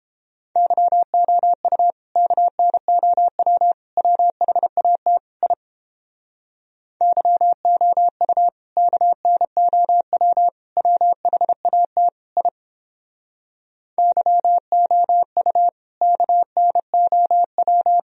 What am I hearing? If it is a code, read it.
Morse code, heard as a keyed tone, about 29 wpm, 700 Hz. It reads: YOU KNOW WHAT I YOU KNOW WHAT I YOU KNOW